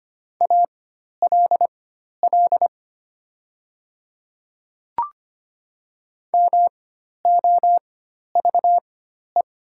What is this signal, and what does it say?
Morse code, heard as a keyed tone, about 25 wpm, spaced as for 12 wpm, 700 Hz. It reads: ALL MOVE
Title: 4x Character Spacing - Top 200 Words - 25wpm